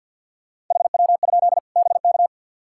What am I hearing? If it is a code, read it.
Morse code, heard as a keyed tone, about 53 wpm, 700 Hz. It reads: HW? BK
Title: Practice QSO - Exchange 1 - Beginning - 50wpm